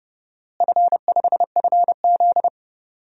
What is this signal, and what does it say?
Morse code, heard as a keyed tone, about 30 wpm, 700 Hz. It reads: F5FZ
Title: DXCC-100 Call Signs - 4 Characters - Repeated 2x - 30wpm